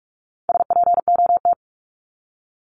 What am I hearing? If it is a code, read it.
Morse code, heard as a keyed tone, about 44 wpm, 700 Hz. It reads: SPOT
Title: Top 1001-1250 Words 45wpm